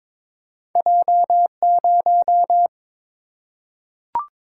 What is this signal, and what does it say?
Morse code, heard as a keyed tone, about 22 wpm, 700 Hz. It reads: J0
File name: practice / 22wpm / Two Letter-Number Combinations - 22wpm